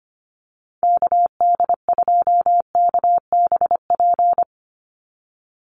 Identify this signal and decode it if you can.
Morse code, heard as a keyed tone, about 25 wpm, 700 Hz. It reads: KD2KBP
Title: USA Call Signs 25wpm